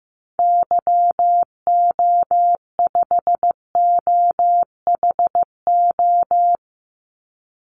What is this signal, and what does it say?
Morse code, heard as a keyed tone, about 15 wpm, 700 Hz. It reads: YO5OHO
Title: Call Signs Repeated 3x - 15wpm